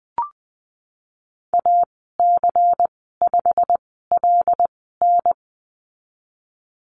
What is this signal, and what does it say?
Morse code, heard as a keyed tone, about 20 wpm, spaced as for 15 wpm, 700 Hz. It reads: AC5LN
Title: 167 Lesson 27g - Number 7 - Call Signs 20wpm